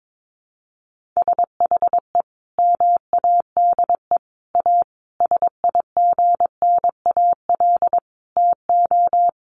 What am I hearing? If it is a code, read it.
Morse code, heard as a keyed tone, about 22 wpm, 700 Hz. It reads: SHE MADE A SIGNAL TO
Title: Sentences from Top 1400 Words 22wpm